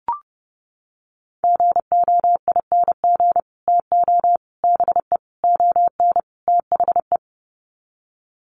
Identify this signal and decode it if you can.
Morse code, heard as a keyed tone, about 30 wpm, 700 Hz. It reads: GOING TO BE ON THE
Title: Sets of 5 Words Encore2 30wpm